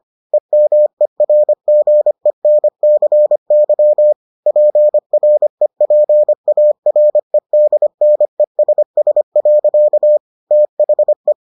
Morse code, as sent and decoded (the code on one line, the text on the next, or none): . -- . .-. --. -. -.-. -.-- / .--. .-. .--. .- .-. -.. -. ... ... .-.-.- / - .... .
EMERGENCY PREPAREDNESS. THE